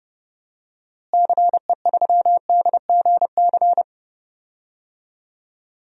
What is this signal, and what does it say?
Morse code, heard as a keyed tone, about 30 wpm, 700 Hz. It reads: CE3DGC
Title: DXCC-100 Call Signs - 6 Characters - Repeated 2x - 30wpm